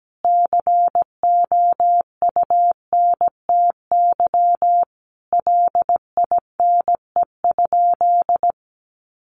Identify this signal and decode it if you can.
Morse code, heard as a keyed tone, about 17 wpm, 700 Hz. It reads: COUNTY LINE?
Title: Sentences from Top 1200 Words 17wpm